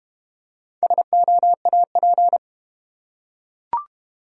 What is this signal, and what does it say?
Morse code, heard as a keyed tone, about 32 wpm, 700 Hz. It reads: SOAP